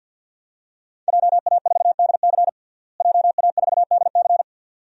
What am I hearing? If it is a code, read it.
Morse code, heard as a keyed tone, about 51 wpm, 700 Hz. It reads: JA4DC JA4DC
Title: DXCC-100 Call Signs - 5 Characters - Repeated 2x - 50wpm